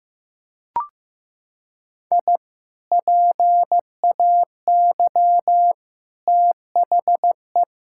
Morse code, heard as a keyed tone, about 15 wpm, 700 Hz. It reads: I PAY THE
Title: Sentences from Top 1000 Words - Review Part 5 - 15wpm